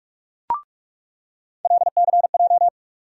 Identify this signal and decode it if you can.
Morse code, heard as a keyed tone, about 46 wpm, 700 Hz. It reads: LCJ
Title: USA Call Sign Postfixes - 3 Letters 45wpm